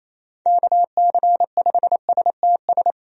Morse code, heard as a keyed tone, about 28 wpm, 700 Hz. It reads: KC5STS
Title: USA Call Signs - 6 Characters - Repeated 3x - 28wpm